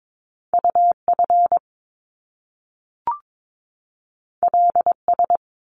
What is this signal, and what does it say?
Morse code, heard as a keyed tone, about 22 wpm, 700 Hz. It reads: UF LS
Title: Top Two Letter Combinations in Words 22wpm